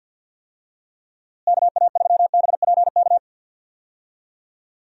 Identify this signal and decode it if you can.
Morse code, heard as a keyed tone, about 51 wpm, 700 Hz. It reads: KA3BPK
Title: USA Call Signs - 6 Characters - Repeated 3x - 50wpm